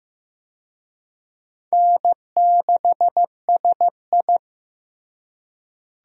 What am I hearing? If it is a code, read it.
Morse code, heard as a keyed tone, about 15 wpm, 700 Hz. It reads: N6SI
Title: USA Call Signs - 4 Characters - Repeated 3x - 15wpm